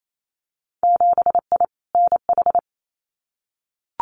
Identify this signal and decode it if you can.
Morse code, heard as a keyed tone, about 28 wpm, 700 Hz. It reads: 7I NH
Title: Winter Field Day Exchanges Encore 28wpm